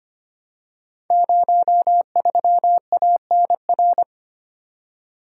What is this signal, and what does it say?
Morse code, heard as a keyed tone, about 25 wpm, 700 Hz. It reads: 03ANR
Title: Code Groups - 5 Letter:Numbers - 25wpm